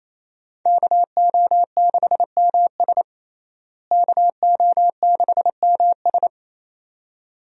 Morse code, heard as a keyed tone, about 28 wpm, 700 Hz. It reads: KO6MS KO6MS
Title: USA Call Signs - 5 Characters - Repeated 2x - 28wpm